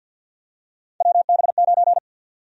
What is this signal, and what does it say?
Morse code, heard as a keyed tone, about 52 wpm, 700 Hz. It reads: WB9
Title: USA Call Sign Prefixes - 50wpm